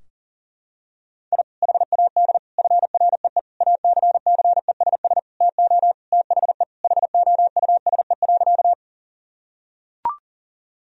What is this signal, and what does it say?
Morse code, heard as a keyed tone, about 40 wpm, 700 Hz. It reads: I HAD FREE ACCESS TO THE HOUSE.
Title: Sentences from Top 1000 Words - Review Part 4 - 40wpm